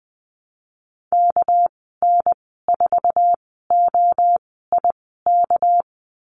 Morse code, heard as a keyed tone, about 20 wpm, spaced as for 14 wpm, 700 Hz. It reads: KN4OIK
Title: 245 Lesson 39g - Letter z - Call Signs 20wpm